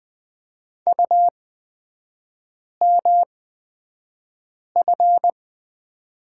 Morse code, heard as a keyed tone, about 20 wpm, spaced as for 5 wpm, 700 Hz. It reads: UMF
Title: 224 Lesson 36e - BT - Three Characters 20wpm